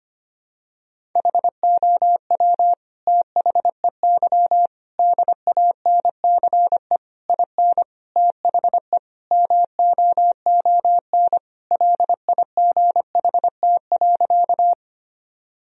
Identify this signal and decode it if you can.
Morse code, heard as a keyed tone, about 25 wpm, 700 Hz. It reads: HOW THEY DANCE IN THE MOON LIGHT.
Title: Sentences from Top 1000 Words - Review Part 4 - 25wpm